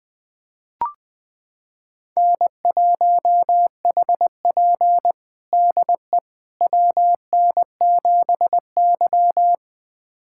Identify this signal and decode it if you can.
Morse code, heard as a keyed tone, about 21 wpm, 700 Hz. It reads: N1HP DE WN7Y
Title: USA Callsign DE Callsign - 4 Characters - 20wpm